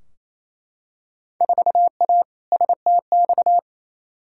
Morse code, heard as a keyed tone, about 27 wpm, 700 Hz. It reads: 4A STX
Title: Field Day Exchanges Encore - 28wpm